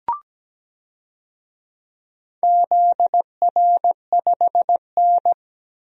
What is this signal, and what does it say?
Morse code, heard as a keyed tone, about 17 wpm, 700 Hz. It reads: ZR5N
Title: DXCC-100 Call Signs - 4 Characters - Repeated 3x - 17wpm